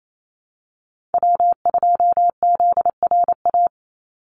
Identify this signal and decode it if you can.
Morse code, heard as a keyed tone, about 28 wpm, 700 Hz. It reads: W2ZRA
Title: Call Signs - 28wpm